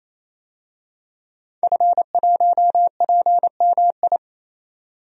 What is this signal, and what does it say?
Morse code, heard as a keyed tone, about 28 wpm, 700 Hz. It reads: F1PMI
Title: DXCC-100 Call Signs - 5 Characters - 28wpm